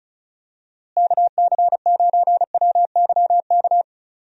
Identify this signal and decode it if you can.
Morse code, heard as a keyed tone, about 36 wpm, 700 Hz. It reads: KC9WYK
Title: USA Call Signs - 6 Characters - Repeated 2x - 35wpm